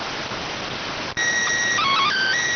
SSTV (slow-scan television), PD 290.